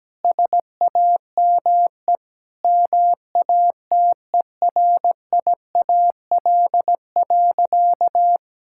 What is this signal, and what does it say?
Morse code, heard as a keyed tone, about 17 wpm, 700 Hz. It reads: SAME MATERIAL.
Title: Sentences from Top 1000 Words - Review Part 5 - 17wpm